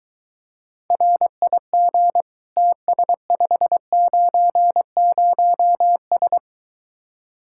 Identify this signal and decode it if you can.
Morse code, heard as a keyed tone, about 23 wpm, 700 Hz. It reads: RIG TS590S